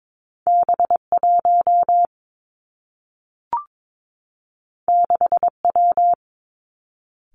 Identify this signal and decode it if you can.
Morse code, heard as a keyed tone, about 22 wpm, 700 Hz. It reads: B 1 6 W
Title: Two Challenging Letter-Number Combinations 22wpm